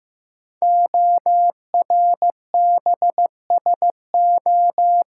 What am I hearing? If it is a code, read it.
Morse code, heard as a keyed tone, about 15 wpm, 700 Hz. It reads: ORBSO
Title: Code Groups - 5 Letter:Numbers - 15wpm